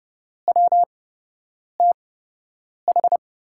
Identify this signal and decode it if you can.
Morse code, heard as a keyed tone, about 32 wpm, spaced as for 8 wpm, 700 Hz. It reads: WTH
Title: USA Call Signs - 6 Characters - 8x Character Spacing - 30wpm